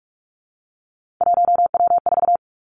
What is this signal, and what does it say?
Morse code, heard as a keyed tone, about 46 wpm, 700 Hz. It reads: 1W4
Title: Three Challenging Letter-Number Combinations 45wpm